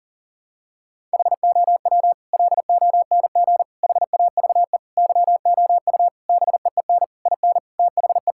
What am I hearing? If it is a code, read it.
Morse code, heard as a keyed tone, about 40 wpm, 700 Hz. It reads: HOW LONG HAVE YOU BEEN IN THE
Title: Sentences from Top 1000 Words - Review Part 6 - 40wpm